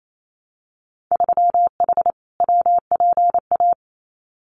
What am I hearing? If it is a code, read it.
Morse code, heard as a keyed tone, about 28 wpm, 700 Hz. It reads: 3H WPA
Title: Winter Field Day Exchanges 28wpm